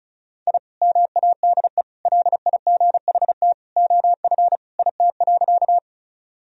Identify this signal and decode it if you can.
Morse code, heard as a keyed tone, about 36 wpm, 700 Hz. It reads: I MADE LIGHT OF IT.
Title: Sentences from Top 1000 Words - Review Part 1 - 35wpm